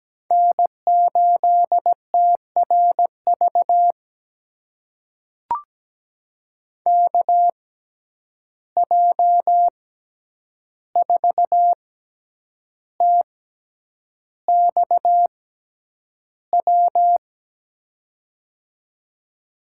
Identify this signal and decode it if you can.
Morse code, heard as a keyed tone, about 17 wpm, spaced as for 8 wpm, 700 Hz. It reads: N8TRV KJ4TXW
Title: USA Call Signs - 6x Character Spacing - 17wpm